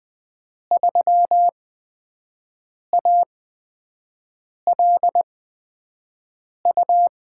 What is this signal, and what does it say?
Morse code, heard as a keyed tone, about 20 wpm, spaced as for 5 wpm, 700 Hz. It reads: 3ALU
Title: USA Call Signs - 6 Characters - 8x Character Spacing - 20wpm